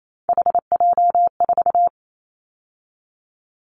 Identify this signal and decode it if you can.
Morse code, heard as a keyed tone, about 28 wpm, 700 Hz. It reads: HJ4
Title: Callsign Prefixes for Top 100 DXCC - 28wpm